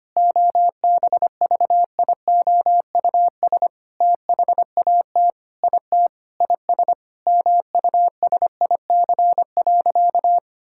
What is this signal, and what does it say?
Morse code, heard as a keyed tone, about 25 wpm, 700 Hz. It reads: OBVIOUS THAT IT IS MUSIC.
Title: Sentences from Top 1600 Words 25wpm